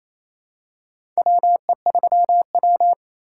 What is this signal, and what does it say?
Morse code, heard as a keyed tone, about 28 wpm, 700 Hz. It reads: WE3W
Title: USA Call Signs - 4 Characters - 2x Character Spacing - 28wpm